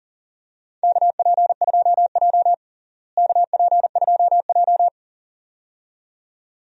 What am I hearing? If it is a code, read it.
Morse code, heard as a keyed tone, about 37 wpm, 700 Hz. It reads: KP2J KP2J
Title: DXCC-100 Call Signs - 4 Characters - Repeated 2x - 40wpm